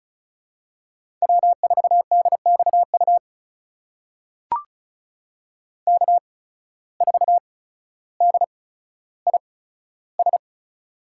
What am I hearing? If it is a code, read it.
Morse code, heard as a keyed tone, about 36 wpm, spaced as for 15 wpm, 700 Hz. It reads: W4DXU K4DIS
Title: USA Call Signs - 5 Characters - 8x Character Spacing - 35wpm